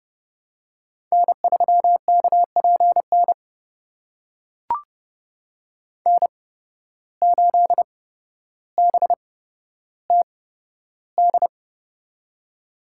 Morse code, heard as a keyed tone, about 30 wpm, spaced as for 13 wpm, 700 Hz. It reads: N3KPN N8BTD
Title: USA Call Signs - 5 Characters - 8x Character Spacing - 30wpm